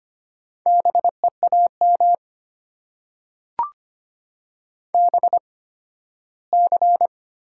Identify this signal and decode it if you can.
Morse code, heard as a keyed tone, about 25 wpm, spaced as for 15 wpm, 690 Hz. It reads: BEAM BC
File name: Top 90 QSO Elements - 8x Character Spacing - 25wpm